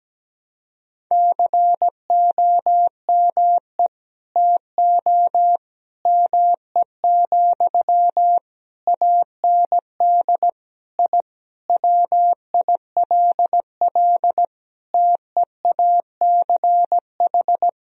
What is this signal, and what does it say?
Morse code, heard as a keyed tone, about 17 wpm, 700 Hz. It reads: COME TO ME, AND I WILL TEACH
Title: Sentences from Top 1000 Words - Review Part 2 - 17wpm